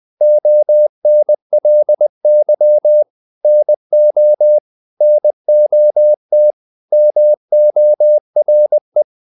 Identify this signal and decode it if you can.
Morse code, heard as a keyed tone, about 20 wpm, 600 Hz. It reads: ONLY NO NOT MORE
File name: CW / Practice / 500MostCommonEnglishWords20